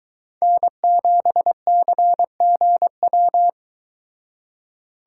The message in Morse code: N7CGW

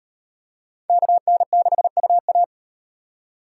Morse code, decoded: KN6UA

